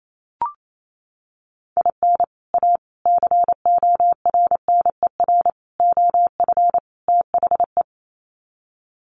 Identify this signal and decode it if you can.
Morse code, heard as a keyed tone, about 28 wpm, 700 Hz. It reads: IN A CORNER OF THE
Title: Sets of 5 Words Encore2 28wpm